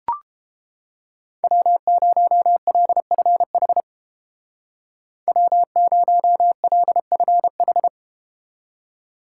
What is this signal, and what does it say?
Morse code, heard as a keyed tone, about 31 wpm, 700 Hz. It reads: W0LFH W0LFH